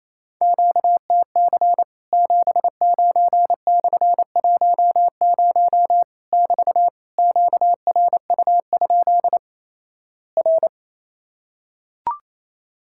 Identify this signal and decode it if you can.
Morse code, heard as a keyed tone, about 28 wpm, 700 Hz. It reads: QTC 79/10 <BT> QRU? R
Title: Worked All Europe DX Contest - QTC Exchange - 28wpm